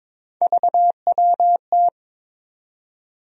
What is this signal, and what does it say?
Morse code, heard as a keyed tone, about 22 wpm, 700 Hz. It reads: VWT
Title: USA Call Sign Postfixes - 3 Letters 22wpm